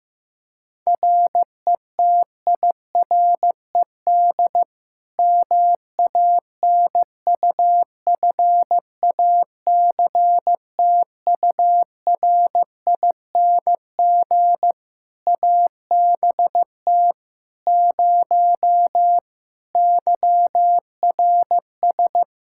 Morse code, heard as a keyed tone, about 15 wpm, 700 Hz. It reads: RETIRED MANUFACTURING ABT 0 YRS
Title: Practice QSO - Exchange 3 - More Info - 15wpm